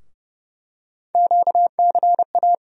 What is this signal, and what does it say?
Morse code, heard as a keyed tone, about 29 wpm, 700 Hz. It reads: QCA
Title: USA Call Sign Postfixes - 3 Letters 30wpm